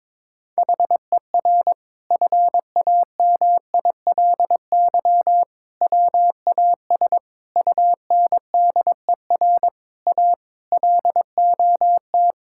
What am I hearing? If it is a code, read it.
Morse code, heard as a keyed tone, about 22 wpm, 700 Hz. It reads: HER FAMILY WAS UNDER A LOT
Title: Sentences from Top 1000 Words - Review Part 6 - 22wpm